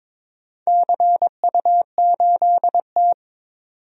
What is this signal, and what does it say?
Morse code, heard as a keyed tone, about 22 wpm, 700 Hz. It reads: CU8T
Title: DXCC-100 Call Signs - 4 Characters - 22wpm